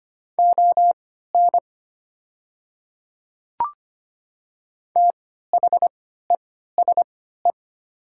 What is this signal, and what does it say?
Morse code, heard as a keyed tone, about 25 wpm, spaced as for 14 wpm, 700 Hz. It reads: ON THESE
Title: 3x Character Spacing - Top 100 Words - 25wpm